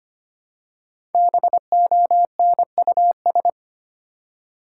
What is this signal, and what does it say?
Morse code, heard as a keyed tone, about 25 wpm, 700 Hz. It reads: BONUS
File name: 5 Letter Words - First 3 Letters Given - 25wpm